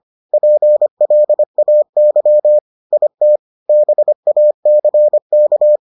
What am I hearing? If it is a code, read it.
Morse code, heard as a keyed tone, about 25 wpm, 600 Hz. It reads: PLAY IT BACK